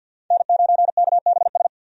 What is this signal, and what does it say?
Morse code, heard as a keyed tone, about 48 wpm, 700 Hz. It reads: N9KBS
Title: USA Call Signs - 5 Characters - 50wpm